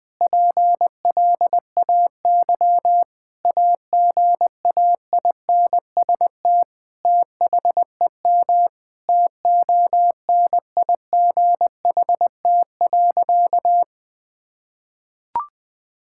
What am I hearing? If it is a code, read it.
Morse code, heard as a keyed tone, about 21 wpm, 700 Hz. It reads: PLAY AGAINST THEM TONIGHT.
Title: Sentences from Top 1000 Words - Review Part 6 - 20wpm